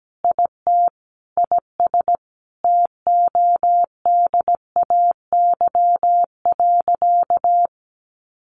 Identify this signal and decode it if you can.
Morse code, heard as a keyed tone, about 17 wpm, 700 Hz. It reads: IT IS TODAY.
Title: Sentences from Top 1500 Words 17wpm